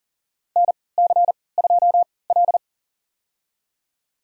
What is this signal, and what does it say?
Morse code, heard as a keyed tone, about 37 wpm, spaced as for 22 wpm, 700 Hz. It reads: NC2L
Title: USA Call Signs - 4 Characters - 3x Character Spacing - 40wpm